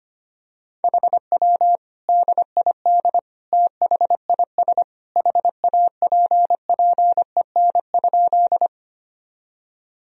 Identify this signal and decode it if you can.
Morse code, heard as a keyed tone, about 25 wpm, 700 Hz. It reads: HW DID THIS HAPPEN?
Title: Sentences from Top 400 Words and Abbreviations 25wpm